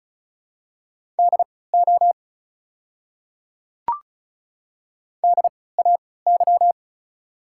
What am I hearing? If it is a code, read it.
Morse code, heard as a keyed tone, about 36 wpm, spaced as for 20 wpm, 690 Hz. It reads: DO DAY
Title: 3x Character Spacing - Top 500 Words - 35wpm